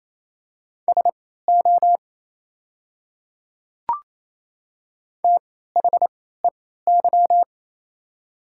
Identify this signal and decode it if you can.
Morse code, heard as a keyed tone, about 27 wpm, spaced as for 16 wpm, 700 Hz. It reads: SO THEY
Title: 3x Character Spacing - Top 100 Words - 28wpm